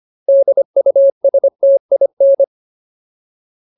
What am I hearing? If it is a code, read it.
Morse code, heard as a keyed tone, about 25 wpm, 550 Hz. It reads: DUSTIN
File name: Ultimate Word List / 25WPM Ultimate